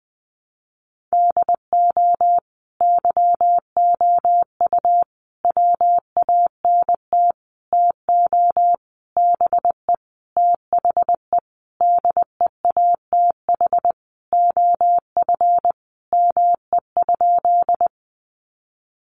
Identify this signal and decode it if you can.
Morse code, heard as a keyed tone, about 20 wpm, 700 Hz. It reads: DO YOU WANT TO BE THE DEATH OF ME?
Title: Sentences from Top 1100 Words 20wpm